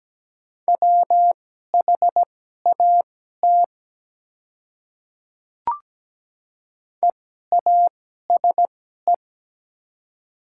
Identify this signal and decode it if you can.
Morse code, heard as a keyed tone, about 17 wpm, spaced as for 12 wpm, 700 Hz. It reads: WHAT EASE
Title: 2x Character Spacing - Top 300 Words - 17wpm